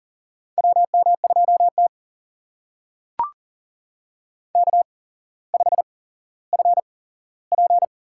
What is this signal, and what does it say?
Morse code, heard as a keyed tone, about 40 wpm, spaced as for 17 wpm, 700 Hz. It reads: WM2T K5FP
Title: USA Call Signs - 4 Characters - 8x Character Spacing - 40wpm